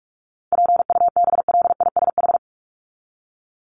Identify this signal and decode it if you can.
Morse code, heard as a keyed tone, about 44 wpm, 700 Hz. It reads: PUBLISH
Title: ED Suffix - Top 100 Words with Base Words 45wpm